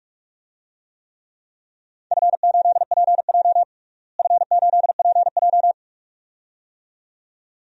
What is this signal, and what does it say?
Morse code, heard as a keyed tone, about 41 wpm, 700 Hz. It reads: F8PJ F8PJ
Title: DXCC-100 Call Signs - 4 Characters - Repeated 2x - 45wpm